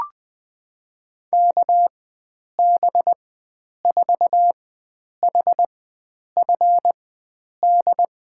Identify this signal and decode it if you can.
Morse code, heard as a keyed tone, about 20 wpm, spaced as for 9 wpm, 700 Hz. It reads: KB4HFD
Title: USA Call Signs - 4x Character Spacing - 20wpm